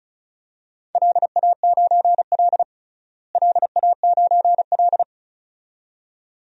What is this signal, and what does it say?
Morse code, heard as a keyed tone, about 36 wpm, 700 Hz. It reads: LA9L LA9L
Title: DXCC-100 Call Signs - 4 Characters - Repeated 2x - 35wpm